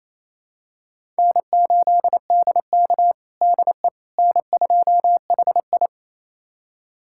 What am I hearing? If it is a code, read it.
Morse code, heard as a keyed tone, about 28 wpm, 700 Hz. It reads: N8DK DE N2HI
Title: USA Callsign DE Callsign - 4 Characters - 28wpm